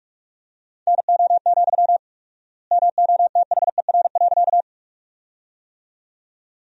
Morse code, heard as a keyed tone, about 42 wpm, 700 Hz. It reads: NO, MOTHER.
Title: 4x Word-Spacing - Sentences from Top 200 Words - 45wpm